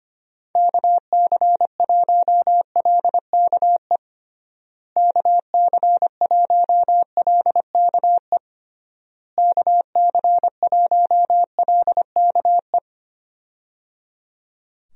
Morse code, heard as a keyed tone, about 25 wpm, 700 Hz. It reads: KC1LKE KC1LKE KC1LKE
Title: USA Call Signs - 6 Characters - Repeated 3x - 25wpm